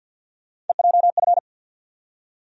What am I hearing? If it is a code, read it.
Morse code, heard as a keyed tone, about 47 wpm, 700 Hz. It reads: EJF
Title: USA Call Sign Postfixes - 3 Letters 50wpm